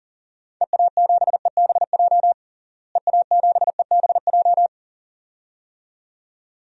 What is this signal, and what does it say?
Morse code, heard as a keyed tone, about 40 wpm, 700 Hz. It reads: EA7EBJ EA7EBJ
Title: DXCC-100 Call Signs - 6 Characters - Repeated 2x - 40wpm